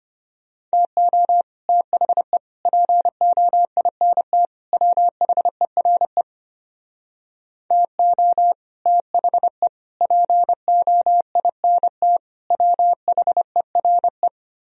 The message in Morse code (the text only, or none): TO THE POINT WHERE TO THE POINT WHERE